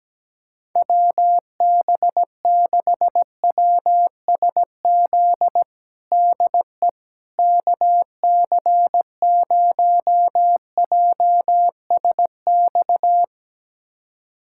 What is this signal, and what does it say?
Morse code, heard as a keyed tone, about 17 wpm, 700 Hz. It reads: WB6WSZ DE KC0JSX
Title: USA Callsign DE Callsign 17wpm